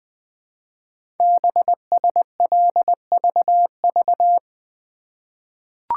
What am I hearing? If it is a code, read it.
Morse code, heard as a keyed tone, about 20 wpm, 700 Hz. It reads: BSLVV
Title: Code Groups - 5 Letters - 20wpm